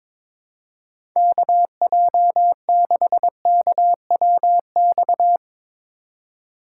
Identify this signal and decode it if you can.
Morse code, heard as a keyed tone, about 22 wpm, 700 Hz. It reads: KJ6KWX
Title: USA Call Signs - 6 Characters - 2x Character Spacing - 22wpm